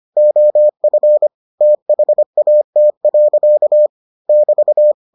Morse code, heard as a keyed tone, about 25 wpm, 600 Hz. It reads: OF THAT. <BT>